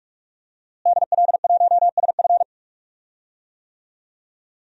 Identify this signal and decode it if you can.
Morse code, heard as a keyed tone, about 41 wpm, 700 Hz. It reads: DL1SF
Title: DXCC-100 Call Signs - 5 Characters - Repeated 2x - 45wpm